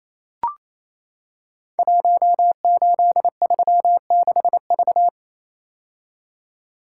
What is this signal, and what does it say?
Morse code, heard as a keyed tone, about 29 wpm, 700 Hz. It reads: 1836V
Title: Code Groups - 5 Letter:Numbers - 28wpm